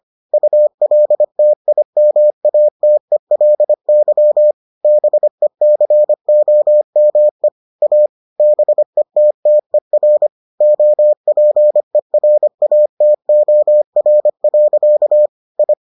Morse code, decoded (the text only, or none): ULTIMATELY BECOME A BETTER OPERATOR. I